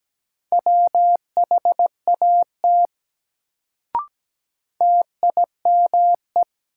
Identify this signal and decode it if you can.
Morse code, heard as a keyed tone, about 17 wpm, 700 Hz. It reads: WHAT TIME
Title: Sets of 2 Words - New Word Drill - 17wpm